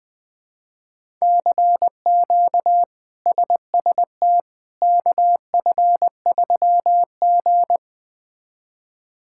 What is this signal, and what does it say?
Morse code, heard as a keyed tone, about 21 wpm, 700 Hz. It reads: CQ SST KF3G
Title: K1USN SST QSO Practice - 20wpm